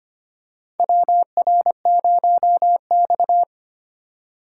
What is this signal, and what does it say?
Morse code, heard as a keyed tone, about 25 wpm, 700 Hz. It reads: WR0X
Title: USA Call Signs - 4 Characters - 4x Character Spacing - 25wpm